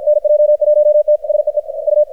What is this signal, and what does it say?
Morse code, heard as a keyed tone, about 50 wpm, 600 Hz, with radio noise. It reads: NJ1T SEE U
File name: archived / CRSnet